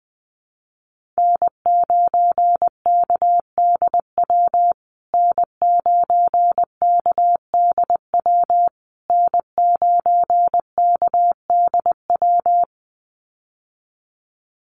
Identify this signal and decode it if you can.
Morse code, heard as a keyed tone, about 20 wpm, 700 Hz. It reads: N9KDW N9KDW N9KDW
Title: USA Call Signs Repeated 3x 20wpm